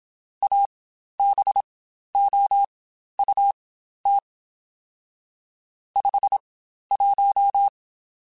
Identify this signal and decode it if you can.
Morse code, heard as a keyed tone, about 27 wpm, spaced as for 12 wpm, 800 Hz. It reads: ABOUT 51
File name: Spaced Characters / Spaced Characters-mp3s / Weather4 21 sp @25